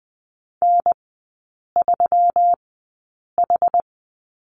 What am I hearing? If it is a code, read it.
Morse code, heard as a keyed tone, about 20 wpm, spaced as for 8 wpm, 700 Hz. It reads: N3H
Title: 121 Lesson 20e - Number 3 - Three Characters 20wpm